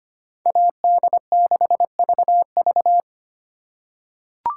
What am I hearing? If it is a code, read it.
Morse code, heard as a keyed tone, about 25 wpm, 700 Hz. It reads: AD6VV
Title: USA Call Signs - 5 Characters - 2x Character Spacing - 25wpm